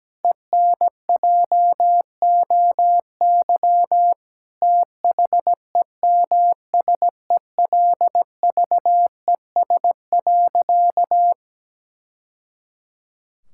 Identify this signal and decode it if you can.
Morse code, heard as a keyed tone, about 17 wpm, 700 Hz. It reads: ENJOY THEMSELVES.
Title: Sentences from Top 1000 Words - Review Part 5 - 17wpm